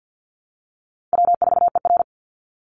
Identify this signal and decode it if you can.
Morse code, heard as a keyed tone, about 50 wpm, 700 Hz. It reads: W4ER
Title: Easy Call Signs - 50wpm